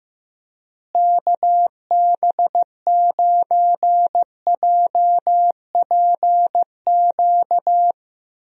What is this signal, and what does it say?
Morse code, heard as a keyed tone, about 15 wpm, 700 Hz. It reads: KB9JPQ
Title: USA Call Signs - 6 Characters - Repeated 3x - 15wpm